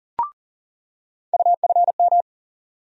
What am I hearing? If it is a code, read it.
Morse code, heard as a keyed tone, about 40 wpm, 700 Hz. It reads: UFM
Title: USA Call Sign Postfixes - 3 Letters 40wpm